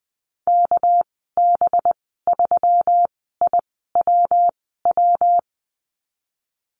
Morse code, heard as a keyed tone, about 20 wpm, spaced as for 15 wpm, 700 Hz. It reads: KB3IWW
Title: 207 Lesson 33g - Letter j - Call Signs 20wpm